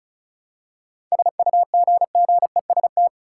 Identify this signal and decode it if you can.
Morse code, heard as a keyed tone, about 36 wpm, 700 Hz. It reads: SUGGEST
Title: Seven Letter Words - 35wpm